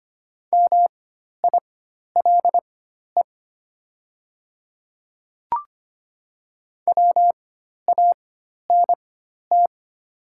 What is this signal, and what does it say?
Morse code, heard as a keyed tone, about 25 wpm, spaced as for 12 wpm, 700 Hz. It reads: MILE WANT
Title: 4x Character Spacing - Top 500 Words - 25wpm